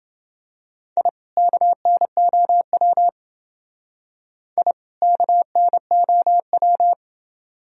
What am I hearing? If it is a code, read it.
Morse code, heard as a keyed tone, about 28 wpm, 700 Hz. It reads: I KNOW I KNOW